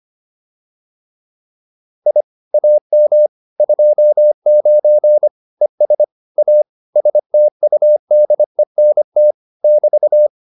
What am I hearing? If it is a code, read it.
Morse code, heard as a keyed tone, about 25 wpm, 600 Hz. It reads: I AM 29 ES A STUDENT <BT>
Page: advanced-practice-files